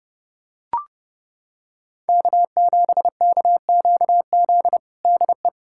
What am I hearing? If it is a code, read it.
Morse code, heard as a keyed tone, about 32 wpm, 700 Hz. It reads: K7KQZ DE
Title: USA Callsign DE Callsign 30wpm